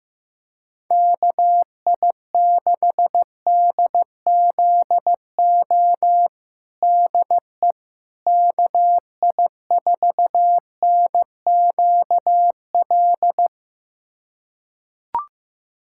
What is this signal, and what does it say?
Morse code, heard as a keyed tone, about 15 wpm, 700 Hz. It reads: KI6DZO DE KI4NQL
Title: USA Callsign DE Callsign 15wpm